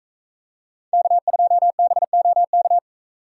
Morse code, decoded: K2BOK

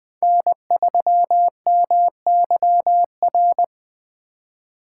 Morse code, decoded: N3MYR

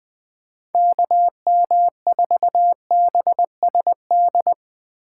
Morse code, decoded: KM4BSD

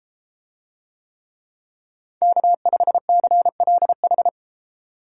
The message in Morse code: K5CLH

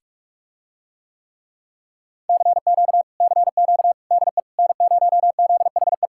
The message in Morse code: CQ CQ DE N0ZHE

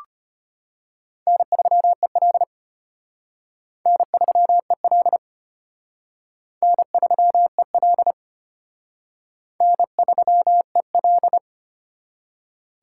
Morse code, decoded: N3EL N3EL N3EL N3EL

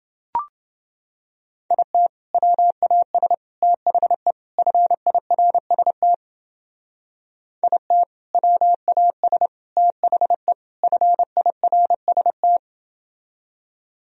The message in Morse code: IT WAS THE FIRST IT WAS THE FIRST

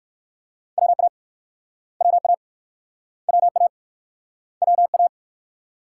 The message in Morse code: WA WA WA WA

